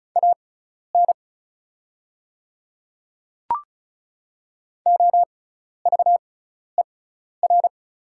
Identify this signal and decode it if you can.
Morse code, heard as a keyed tone, about 34 wpm, spaced as for 12 wpm, 700 Hz. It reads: AN OVER
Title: 6x Character Spacing - Top 100 Words - 35wpm